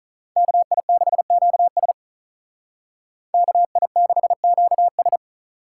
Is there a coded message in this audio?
Yes: Morse code from 0.4 to 5.2 s, heard as a keyed tone, about 45 wpm, 700 Hz. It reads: KI6QS KI6QS